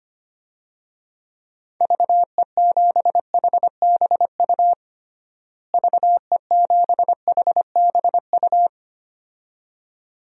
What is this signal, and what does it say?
Morse code, heard as a keyed tone, about 25 wpm, 700 Hz. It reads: VE7HBU VE7HBU
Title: DXCC-100 Call Signs - 6 Characters - Repeated 2x - 25wpm